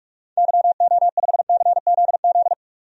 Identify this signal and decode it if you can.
Morse code, heard as a keyed tone, about 48 wpm, 700 Hz. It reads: YO5CZZ
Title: DXCC-100 Call Signs - 6 Characters - 45wpm